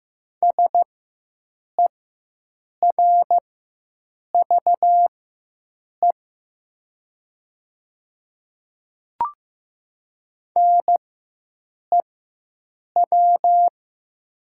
Morse code, heard as a keyed tone, about 15 wpm, spaced as for 7 wpm, 700 Hz. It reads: SERVE NEW
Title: 4x Character Spacing - Top 500 Words - 15wpm